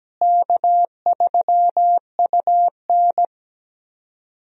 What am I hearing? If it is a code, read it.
Morse code, heard as a keyed tone, about 17 wpm, 700 Hz. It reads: K3UN
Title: USA Call Signs - 4 Characters - Repeated 2x - 17wpm